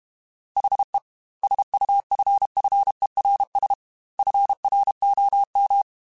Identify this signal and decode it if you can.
Morse code, heard as a keyed tone, about 32 wpm, 800 Hz. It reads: HE SUFFERS FROM
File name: Quotes / Quotes-mp3s / HoofInMouth @30